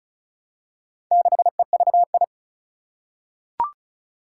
Morse code, heard as a keyed tone, about 36 wpm, 700 Hz. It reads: 6EVI
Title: Four Challenging Letter-Number Combinations 35wpm